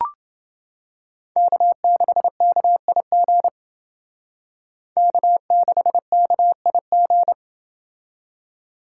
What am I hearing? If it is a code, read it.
Morse code, heard as a keyed tone, about 30 wpm, 700 Hz. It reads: K6KIG K6KIG